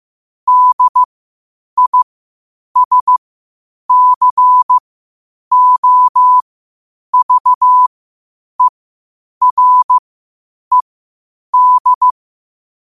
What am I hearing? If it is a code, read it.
Morse code, heard as a keyed tone, about 15 wpm, spaced as for 8 wpm, 1000 Hz. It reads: DISCOVERED
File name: qotdfiles / 8wpm